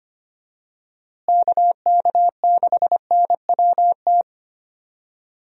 Morse code, heard as a keyed tone, about 25 wpm, 700 Hz. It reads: KK6NWT
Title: USA Call Signs - 6 Characters - Repeated 2x - 25wpm